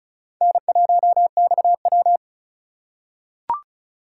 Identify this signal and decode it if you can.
Morse code, heard as a keyed tone, about 36 wpm, 700 Hz. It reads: N1XW
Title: USA Call Signs - 4 Characters - 2x Character Spacing - 35wpm